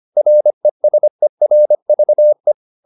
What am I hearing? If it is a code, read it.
Morse code, heard as a keyed tone, about 25 wpm, 600 Hz. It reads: RESERVE
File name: the old man and the sea part 1 25wpm